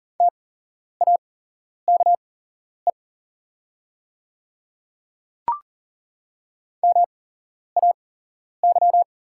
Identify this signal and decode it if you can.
Morse code, heard as a keyed tone, about 42 wpm, spaced as for 11 wpm, 700 Hz. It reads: TAKE MAY
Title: 8x Character Spacing - Top 200 Words - 40wpm